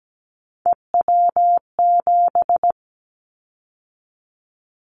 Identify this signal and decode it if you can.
Morse code, heard as a keyed tone, about 17 wpm, 700 Hz. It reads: EW7
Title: Callsign Prefixes for Top 50 DXCC - 17wpm